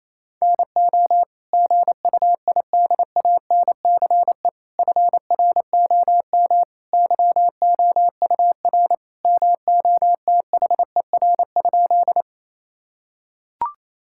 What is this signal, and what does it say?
Morse code, heard as a keyed tone, about 28 wpm, 700 Hz. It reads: NO GUIDANCE FROM YOUR MOTHER?
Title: Sentences from Top 1600 Words 28wpm